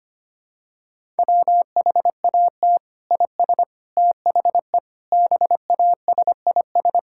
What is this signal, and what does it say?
Morse code, heard as a keyed tone, about 25 wpm, 700 Hz. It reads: WHAT IS THE BASIS
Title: Sentences from Top 1000 Words - Review Part 5 - 25wpm